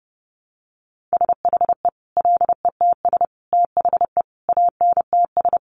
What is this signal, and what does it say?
Morse code, heard as a keyed tone, about 30 wpm, 700 Hz. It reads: SHE LETS THE ANTS
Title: Additional Easy Phrases 30wpm